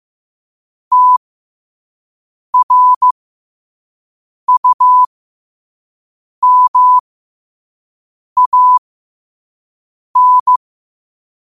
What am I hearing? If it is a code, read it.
Morse code, heard as a keyed tone, about 15 wpm, spaced as for 5 wpm, 1000 Hz. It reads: TRUMAN
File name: qotdfiles / 5wpm